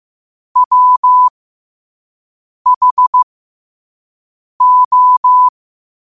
Morse code, heard as a keyed tone, about 15 wpm, spaced as for 5 wpm, 1000 Hz. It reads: WHO
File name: qotdfiles / 5wpm